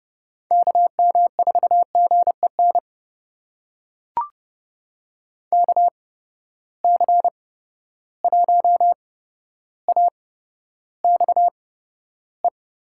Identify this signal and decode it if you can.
Morse code, heard as a keyed tone, about 30 wpm, spaced as for 13 wpm, 700 Hz. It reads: KM4GEN KC1AXE
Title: USA Call Signs - 6 Characters - 8x Character Spacing - 30wpm